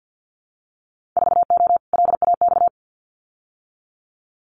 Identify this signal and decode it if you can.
Morse code, heard as a keyed tone, about 50 wpm, 700 Hz. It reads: 4O LAX
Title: Winter Field Day Exchanges Encore 50wpm